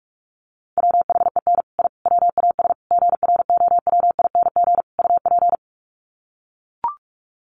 Morse code, heard as a keyed tone, about 44 wpm, 700 Hz. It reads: WHEN I WAS GROWING UP
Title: Sets of 5 Words Encore 45wpm